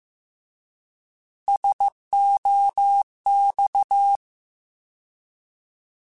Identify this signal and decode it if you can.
Morse code, heard as a keyed tone, about 15 wpm, 800 Hz. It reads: SOX
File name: Alphabetical / Alphabetical mp3s / Ss @13